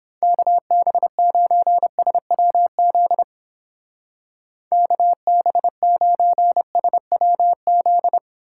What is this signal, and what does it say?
Morse code, heard as a keyed tone, about 27 wpm, 700 Hz. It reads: KB9SWZ KB9SWZ
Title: Speed Racing - USA Call Signs 22wpm